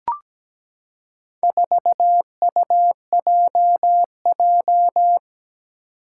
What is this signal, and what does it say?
Morse code, heard as a keyed tone, about 17 wpm, 700 Hz. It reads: 4UJJ
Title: Four Challenging Letter-Number Combinations 17wpm